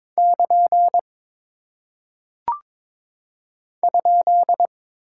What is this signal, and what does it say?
Morse code, heard as a keyed tone, about 22 wpm, 700 Hz. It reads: <KN> ?
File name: practice / 22wpm / Single Punctuation-Prosign - 22wpm